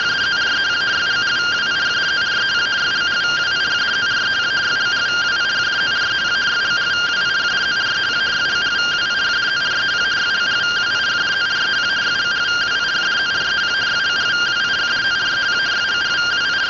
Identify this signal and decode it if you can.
Two-tone teleprinter signal, about 170 Hz shift.
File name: findings / axial iodine 10mm 184.gen-26 rtty rx